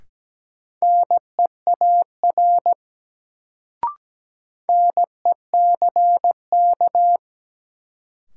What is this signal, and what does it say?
Morse code, heard as a keyed tone, about 17 wpm, 700 Hz. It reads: NEAR NECK
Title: Sets of 5 Words Encore - New Word Drill - 17wpm